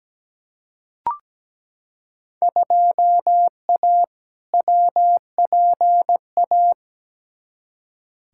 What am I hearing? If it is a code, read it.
Morse code, heard as a keyed tone, about 17 wpm, 700 Hz. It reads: 2A WPA
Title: Field Day Exchanges - 17wpm